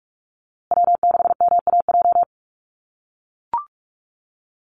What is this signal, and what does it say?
Morse code, heard as a keyed tone, about 44 wpm, 700 Hz. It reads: W6MAJ
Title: USA Call Signs Repeated 3x 45wpm